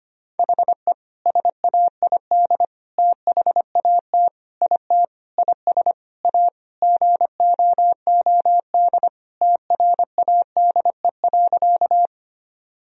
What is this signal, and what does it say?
Morse code, heard as a keyed tone, about 25 wpm, 700 Hz. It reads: HE SAID THAT IT IS A GOOD TRADE.
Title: Sentences from Top 1000 Words - Review Part 4 - 25wpm